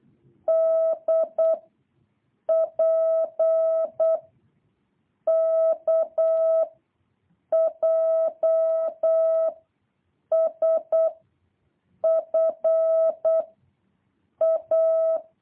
Morse code, heard as a keyed tone, about 8 wpm, 660 Hz. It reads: D P K J S F A
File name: jh8cbh / 2023cw